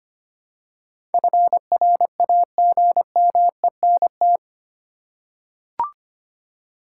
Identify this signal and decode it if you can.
Morse code, heard as a keyed tone, about 25 wpm, 700 Hz. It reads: FRAGMENT
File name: 8 Letter Words - First 3 Letters Given - 25wpm